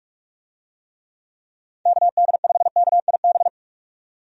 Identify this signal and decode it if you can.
Morse code, heard as a keyed tone, about 46 wpm, 700 Hz. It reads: KD5KIB
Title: USA Call Signs - 6 Characters - 45wpm